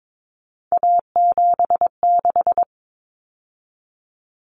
Morse code, heard as a keyed tone, about 22 wpm, 700 Hz. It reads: A76
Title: Common Cut Numbers - 3-Digit 22wpm